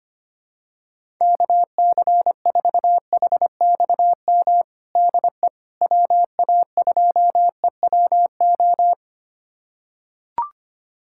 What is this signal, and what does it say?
Morse code, heard as a keyed tone, about 25 wpm, 700 Hz. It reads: KC4HXM DE WA2EWO
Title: USA Callsign DE Callsign 25wpm